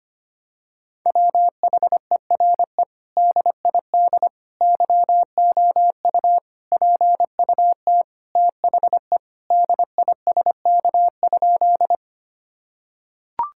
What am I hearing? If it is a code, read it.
Morse code, heard as a keyed tone, about 25 wpm, 700 Hz. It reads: WHERE DID YOU PUT THE DISK?
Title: Sentences from Top 1000 Words - Review Part 6 - 25wpm